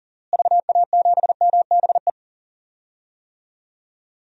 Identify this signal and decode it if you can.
Morse code, heard as a keyed tone, about 40 wpm, 700 Hz. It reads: VA7MBE
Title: DXCC-100 Call Signs - 6 Characters - Repeated 2x - 40wpm